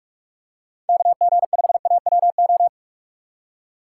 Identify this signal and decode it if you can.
Morse code, heard as a keyed tone, about 46 wpm, 700 Hz. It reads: KG5AWO
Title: USA Call Signs - 6 Characters - 2x Character Spacing - 45wpm